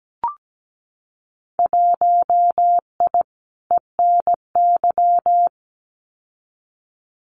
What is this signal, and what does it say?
Morse code, heard as a keyed tone, about 17 wpm, 700 Hz. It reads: 1I ENY
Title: Winter Field Day Exchanges Encore 17wpm